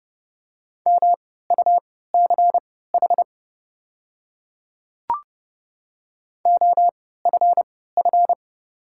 Morse code, heard as a keyed tone, about 29 wpm, spaced as for 17 wpm, 700 Hz. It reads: MUCH OFF
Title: 3x Character Spacing - Top 300 Words - 30wpm